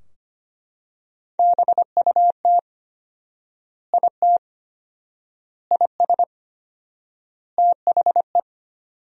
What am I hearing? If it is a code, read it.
Morse code, heard as a keyed tone, about 25 wpm, 700 Hz. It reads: BUT IT IS THE
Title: 4x Word-Spacing - Sentences from Top 300 Words - 25wpm